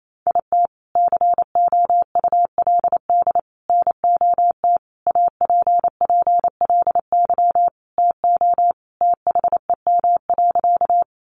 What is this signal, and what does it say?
Morse code, heard as a keyed tone, about 28 wpm, 700 Hz. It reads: IT COULD NOT APPLY TO THEM.
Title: Sentences from Top 1200 Words 28wpm